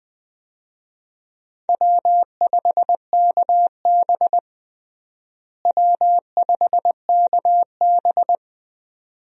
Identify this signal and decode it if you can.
Morse code, heard as a keyed tone, about 21 wpm, 700 Hz. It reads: W5KB W5KB
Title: USA Call Signs - 4 Characters - Repeated 2x - 20wpm